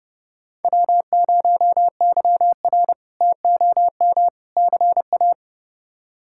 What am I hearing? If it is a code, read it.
Morse code, heard as a keyed tone, about 30 wpm, 700 Hz. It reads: W0YR TOM CA
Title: K1USN SST QSO Practice - 30wpm